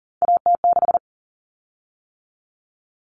Morse code, heard as a keyed tone, about 40 wpm, 700 Hz. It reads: AT6
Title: Common Cut Numbers - 3-Digit 40wpm